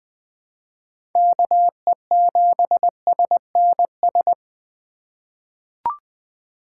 Morse code, heard as a keyed tone, about 20 wpm, 700 Hz. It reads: KE7SNS
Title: USA Call Signs - 6 Characters - 3x Character Spacing - 20wpm